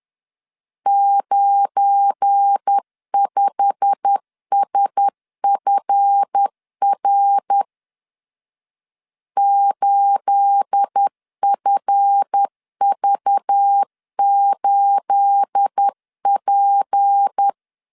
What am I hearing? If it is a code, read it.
Morse code, heard as a keyed tone, about 10 wpm, 800 Hz. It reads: 95SFR 8FV8P
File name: CW / DET / 1 9 1500C S B 10wpm 4439